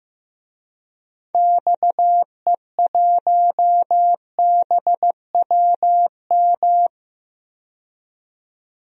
Morse code, heard as a keyed tone, about 15 wpm, 700 Hz. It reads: XE1BWM
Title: DXCC-100 Call Signs - 6 Characters - 15wpm